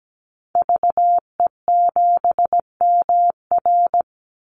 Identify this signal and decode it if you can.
Morse code, heard as a keyed tone, about 17 wpm, 700 Hz. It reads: VE7MR
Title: Call Signs Repeated 2x - 17wpm